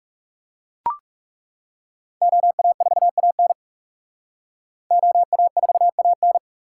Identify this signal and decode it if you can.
Morse code, heard as a keyed tone, about 35 wpm, 700 Hz. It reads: OA4AN OA4AN
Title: Speed Racing - USA 2x2 Call Signs 30wpm